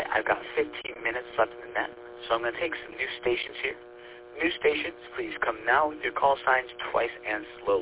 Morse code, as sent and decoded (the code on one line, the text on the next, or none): none